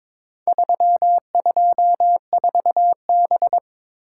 Morse code, heard as a keyed tone, about 22 wpm, 700 Hz. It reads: 324B
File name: practice / 22wpm / Four Letter-Number Combinations - 22wpm